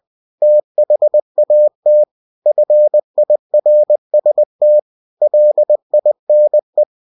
Morse code, heard as a keyed tone, about 20 wpm, 600 Hz. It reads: THAT FIRST LINE